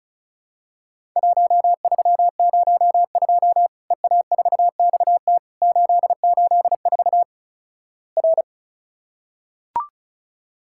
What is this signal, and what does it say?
Morse code, heard as a keyed tone, about 36 wpm, 700 Hz. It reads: 1302 EA4XT 884 R
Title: Worked All Europe DX Contest - QTC Exchange - 35wpm